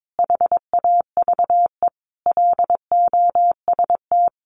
Morse code, heard as a keyed tone, about 22 wpm, 700 Hz. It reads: HAVE LOST
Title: Sentences from Top 1400 Words 22wpm